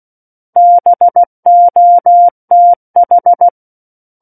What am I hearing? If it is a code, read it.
Morse code, heard as a keyed tone, about 16 wpm, 700 Hz. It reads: BOTH
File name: Book One / 07-Chapter 7 - 16 WPM